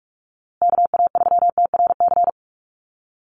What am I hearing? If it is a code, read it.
Morse code, heard as a keyed tone, about 44 wpm, 690 Hz. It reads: KA3TRC
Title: USA Call Signs Repeated 3x 45wpm